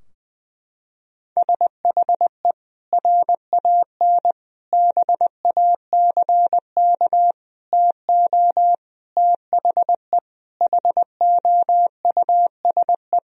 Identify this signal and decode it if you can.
Morse code, heard as a keyed tone, about 20 wpm, 700 Hz. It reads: SHE RAN BACK TO THE HOUSE
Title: Sentences from Top 1000 Words - Review Part 3 - 20wpm